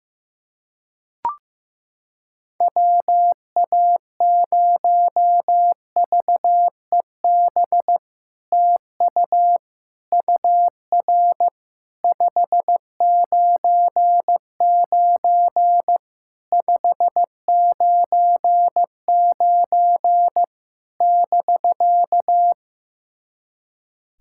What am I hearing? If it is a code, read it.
Morse code, heard as a keyed tone, about 15 wpm, 700 Hz. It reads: WA0VEB TU UR 599 599 <BK>
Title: SOTA QSO Practice 15wpm